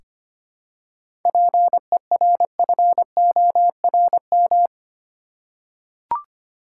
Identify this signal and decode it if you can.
Morse code, heard as a keyed tone, about 25 wpm, 700 Hz. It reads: PERFORM
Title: Seven Letter Words - 25wpm